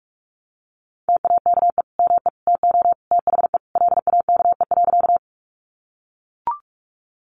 Morse code, heard as a keyed tone, about 44 wpm, 700 Hz. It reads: TAKE ME TO THE LAKE.
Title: Sentences from Top 1100 Words 45wpm